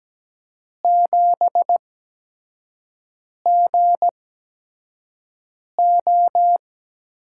Morse code, heard as a keyed tone, about 17 wpm, spaced as for 5 wpm, 700 Hz. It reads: 7GO
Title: USA Call Signs - 8x Character Spacing - 17wpm